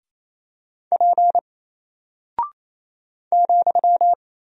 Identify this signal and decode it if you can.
Morse code, heard as a keyed tone, about 27 wpm, 700 Hz. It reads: P ,
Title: Single Character (Letter:Number:Punctuation:Prosign) 28wpm